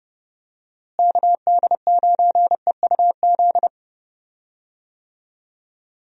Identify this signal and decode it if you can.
Morse code, heard as a keyed tone, about 30 wpm, 700 Hz. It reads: KD9EUZ
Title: USA Call Signs - 6 Characters - Repeated 3x - 30wpm